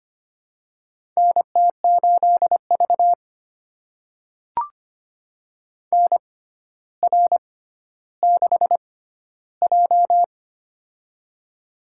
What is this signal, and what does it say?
Morse code, heard as a keyed tone, about 25 wpm, spaced as for 13 wpm, 700 Hz. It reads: NT8V NR6J
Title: USA Call Signs - 4 Characters - 6x Character Spacing - 25wpm